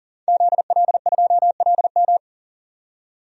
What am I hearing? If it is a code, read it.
Morse code, heard as a keyed tone, about 40 wpm, 700 Hz. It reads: ZL2LM
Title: DXCC-100 Call Signs - 5 Characters - 40wpm